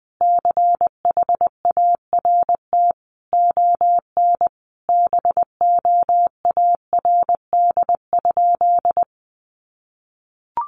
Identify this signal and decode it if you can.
Morse code, heard as a keyed tone, about 20 wpm, 700 Hz. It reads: CHART ON BOARD?
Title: Sentences from Top 1300 Words 20wpm